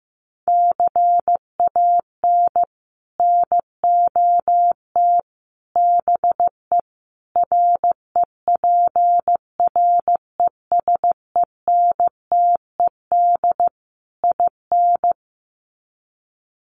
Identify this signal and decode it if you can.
Morse code, heard as a keyed tone, about 14 wpm, 700 Hz. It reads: CAN NOT BE REPRESENTED IN
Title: Sets of 5 Words 15wpm